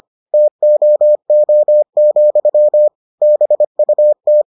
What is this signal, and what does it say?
Morse code, heard as a keyed tone, about 25 wpm, 600 Hz. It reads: TOO, BUT